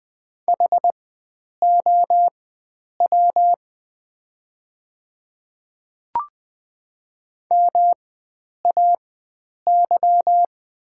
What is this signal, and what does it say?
Morse code, heard as a keyed tone, about 20 wpm, spaced as for 9 wpm, 690 Hz. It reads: HOW MAY
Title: 4x Character Spacing - Top 100 Words - 20wpm